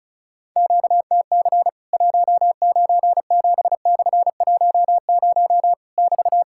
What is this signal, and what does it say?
Morse code, heard as a keyed tone, about 36 wpm, 700 Hz. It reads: QTC 197/10 <BT>
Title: Worked All Europe DX Contest - QTC Exchange - 35wpm